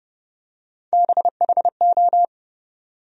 Morse code, heard as a keyed tone, about 32 wpm, 700 Hz. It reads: BHO
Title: USA Call Sign Postfixes - 3 Letters 30wpm